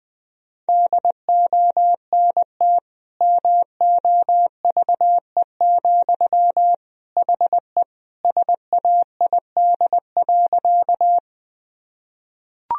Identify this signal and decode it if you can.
Morse code, heard as a keyed tone, about 20 wpm, 700 Hz. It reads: DONT MOVE, HE SAID.
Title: Sentences from Top 1000 Words - Review Part 1 - 20wpm